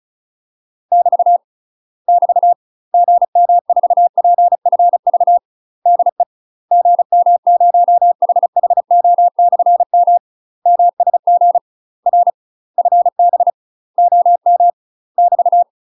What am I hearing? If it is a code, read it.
Morse code, heard as a keyed tone, about 34 wpm, 700 Hz. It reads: <BT> <BT> GM4PFV DE GM0HHO/M MSG R FB OM <BT>
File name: UK MASTER QSOs TRAINING / 35WPM / UK MASTER QSOs 0037